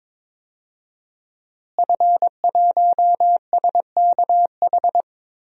Morse code, heard as a keyed tone, about 22 wpm, 700 Hz. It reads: F1SKH
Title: DXCC-100 Call Signs - 5 Characters - Repeated 3x - 22wpm